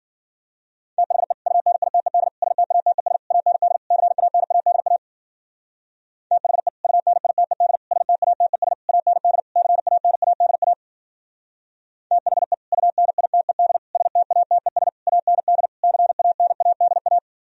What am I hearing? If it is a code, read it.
Morse code, heard as a keyed tone, about 51 wpm, 700 Hz. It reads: THE UNITED STATES AND CANADA THE UNITED STATES AND CANADA THE UNITED STATES AND CANADA